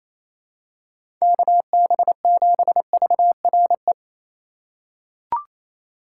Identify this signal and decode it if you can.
Morse code, heard as a keyed tone, about 28 wpm, 700 Hz. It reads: KB7VRE
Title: USA Call Signs - 6 Characters - 2x Character Spacing - 28wpm